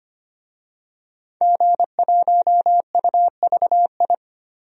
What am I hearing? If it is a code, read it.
Morse code, heard as a keyed tone, about 25 wpm, 700 Hz. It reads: G1UVI
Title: DXCC-100 Call Signs - 5 Characters - 25wpm